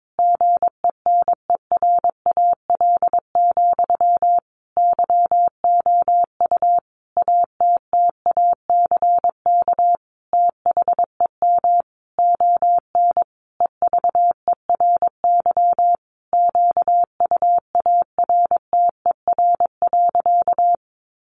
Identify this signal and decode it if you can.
Morse code, heard as a keyed tone, about 22 wpm, 700 Hz. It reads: GENERAL, YOU ATTACK THEM ON EVERY QUARTER.
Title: Sentences from Top 1500 Words 22wpm